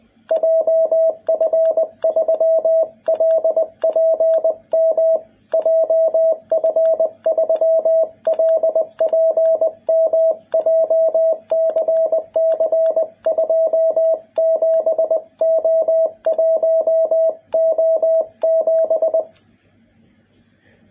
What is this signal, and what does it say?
Morse code, heard as a keyed tone, about 20 wpm, 650 Hz, with radio noise. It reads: JF3LPM JF3LPM JCC27O1O7
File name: jh8cbh / 2021cw